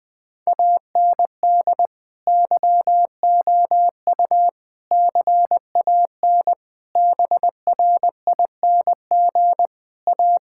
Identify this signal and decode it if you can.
Morse code, heard as a keyed tone, about 20 wpm, 700 Hz. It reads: AND YOU CAN BRING A
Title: Sentences with 2-Word Phrases - 20wpm